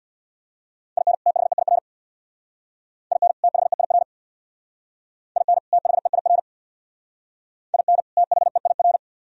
Morse code, heard as a keyed tone, about 61 wpm, 700 Hz. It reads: IN THEIR IN THEIR IN THEIR IN THEIR
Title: Speed-Racing - Sets of 2 Words 50wpm